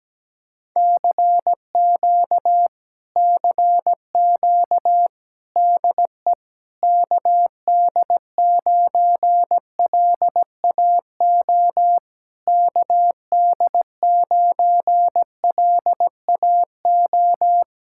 Morse code, heard as a keyed tone, about 17 wpm, 700 Hz. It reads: CQ CQ DE KD9LAO KD9LAO
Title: Speed-Racing Common QSO Phrases - 17wpm